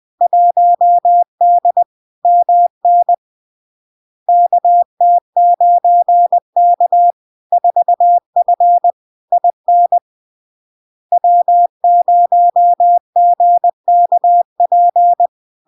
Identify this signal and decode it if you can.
Morse code, heard as a keyed tone, about 20 wpm, 700 Hz. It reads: 1D MN KT9K 4F IN W0GKP
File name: LARK-FD-2022-Log-in-CW-20wpm-700-Hz